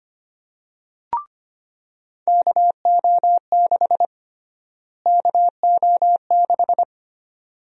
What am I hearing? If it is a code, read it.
Morse code, heard as a keyed tone, about 25 wpm, 700 Hz. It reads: KO6 KO6
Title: USA Call Sign Prefixes - Repeated 2x - 25wpm